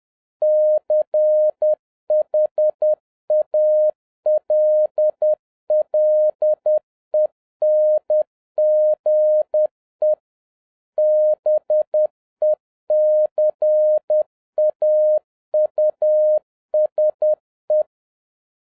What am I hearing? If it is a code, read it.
Morse code, heard as a keyed tone, about 10 wpm, 610 Hz. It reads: CHALLENGE BECAUSE
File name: 2011-07-mp3 / n21 GREETINGS FROM SINGAPORE 10wpm